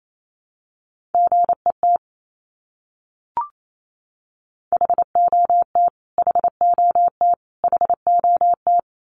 Morse code, heard as a keyed tone, about 28 wpm, 700 Hz. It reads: GET HOT HOT HOT
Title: Top 200 Words Repeated 28wpm